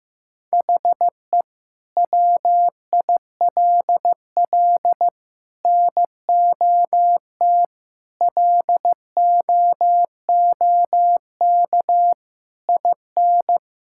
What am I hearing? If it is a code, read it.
Morse code, heard as a keyed tone, about 15 wpm, 700 Hz. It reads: HE WILL NOT LOOK IN
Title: Sentences from Top 1000 Words - Review Part 5 - 15wpm